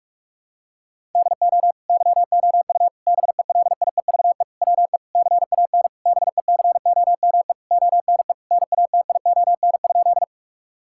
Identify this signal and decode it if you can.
Morse code, heard as a keyed tone, about 46 wpm, 700 Hz. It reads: DO YOU BELIEVE WE CAN BECOME ONE NATION?
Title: Sentences from Top 1000 Words - Review Part 6 - 45wpm